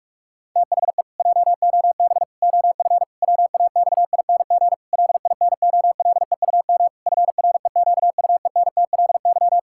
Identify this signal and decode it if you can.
Morse code, heard as a keyed tone, about 41 wpm, 700 Hz. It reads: THE JOB OF WAXING LINOLEUM FREQUENTLY
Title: Pangram Challenge v2 - 45wpm